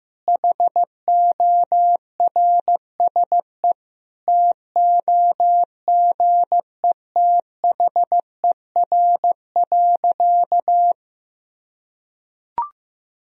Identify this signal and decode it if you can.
Morse code, heard as a keyed tone, about 15 wpm, 700 Hz. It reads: HORSE TOGETHER.
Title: Sentences from Top 1000 Words - Review Part 1 - 15wpm